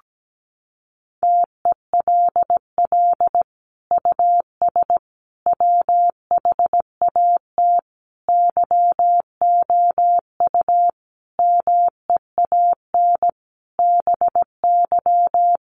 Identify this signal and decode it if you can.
Morse code, heard as a keyed tone, about 17 wpm, 700 Hz. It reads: TELL US WHAT YOU MEAN BY
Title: Sentences from Top 1000 Words 17wpm